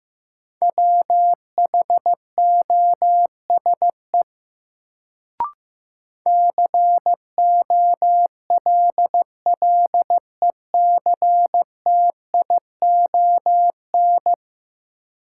Morse code, heard as a keyed tone, about 15 wpm, 700 Hz. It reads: WHOSE COLLECTION
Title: Sentences from Top 1200 Words - New Word Drill Alt - 15wpm